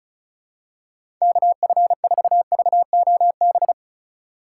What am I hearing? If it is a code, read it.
Morse code, heard as a keyed tone, about 36 wpm, 700 Hz. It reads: KF4VOB
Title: USA Call Signs - 6 Characters - 3x Character Spacing - 35wpm